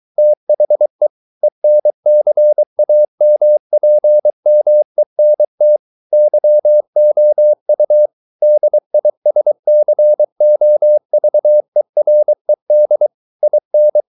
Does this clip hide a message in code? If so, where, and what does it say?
Yes: Morse code from 0.2 to 14.0 s, heard as a keyed tone, about 23 wpm, 600 Hz. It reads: THE ENCAMPMENT YOU DISCOVERED IN